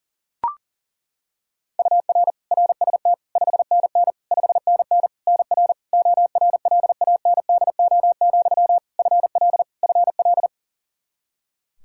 Morse code, heard as a keyed tone, about 40 wpm, 700 Hz. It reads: UR RST 5NN 5NN NR ORLANDO, FL FL
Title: Practice QSO - Exchange 1 - Beginning - 40wpm